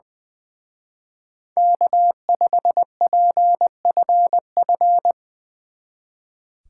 Morse code, heard as a keyed tone, about 19 wpm, 700 Hz. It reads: K5PFF
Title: Speed Racing - USA Call Signs 20wpm